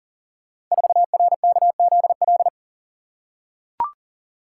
Morse code, heard as a keyed tone, about 42 wpm, 690 Hz. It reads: 4RKZL
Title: Code Groups - 5 Letter:Numbers - 40wpm